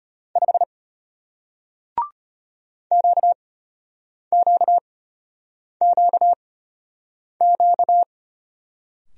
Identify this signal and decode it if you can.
Morse code, heard as a keyed tone, about 34 wpm, spaced as for 8 wpm, 700 Hz. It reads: 5 QQQQ